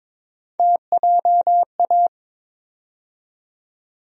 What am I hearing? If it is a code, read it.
Morse code, heard as a keyed tone, about 22 wpm, 700 Hz. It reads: TJA
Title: USA Call Sign Postfixes - 3 Letters 22wpm